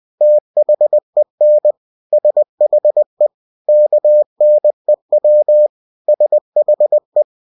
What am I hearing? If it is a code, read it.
Morse code, heard as a keyed tone, about 20 wpm, 600 Hz. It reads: THEN SHE KNEW SHE
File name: cw-books / winnie-the-pooh / winnie-20wpm-page-0071